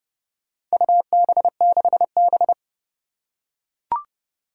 Morse code, heard as a keyed tone, about 28 wpm, 700 Hz. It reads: UB6B
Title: Four Challenging Letter-Number Combinations 30wpm